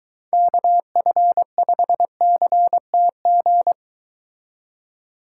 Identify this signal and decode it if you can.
Morse code, heard as a keyed tone, about 22 wpm, 700 Hz. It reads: KF5CTG